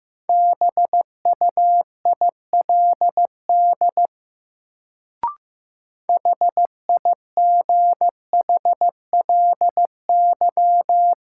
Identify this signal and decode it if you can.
Morse code, heard as a keyed tone, about 15 wpm, 700 Hz. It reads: BUILD HIGHLY
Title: Sentences from Top 1100 Words - New Word Drill Alt - 15wpm